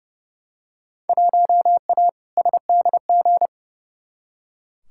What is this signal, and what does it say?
Morse code, heard as a keyed tone, about 30 wpm, 700 Hz. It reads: 1A SDG
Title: Field Day Exchanges Encore - 30wpm